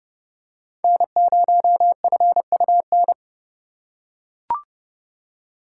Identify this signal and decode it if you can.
Morse code, heard as a keyed tone, about 29 wpm, 700 Hz. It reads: N0FUN